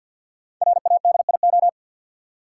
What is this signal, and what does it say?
Morse code, heard as a keyed tone, about 49 wpm, 700 Hz. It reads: RADIO
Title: Five Letter Words - 50wpm